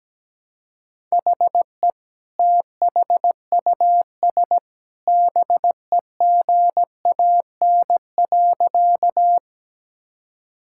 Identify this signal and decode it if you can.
Morse code, heard as a keyed tone, about 17 wpm, 700 Hz. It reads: HE THUS BEGAN.
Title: Sentences from Top 1000 Words - Review Part 5 - 17wpm